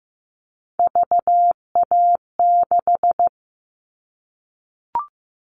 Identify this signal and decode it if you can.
Morse code, heard as a keyed tone, about 15 wpm, 700 Hz. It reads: VA6
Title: All Cut Numbers - 3-Digit 15wpm